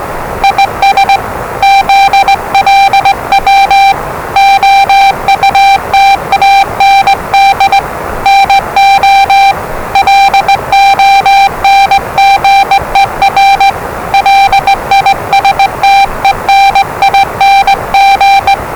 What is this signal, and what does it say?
Morse code, heard as a keyed tone, about 20 wpm, 790 Hz, with noise. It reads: IS ZLW OUT AND NO LONGER LISTENING
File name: WellingtonRadioZlwCloseDownOnHfAndMf / 01-Zlw 4277khz 417.5 qrt until 0925z